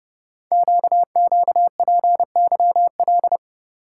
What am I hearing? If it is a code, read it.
Morse code, heard as a keyed tone, about 30 wpm, 700 Hz. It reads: QQPYL
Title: Code Groups - 5 Letters - 30wpm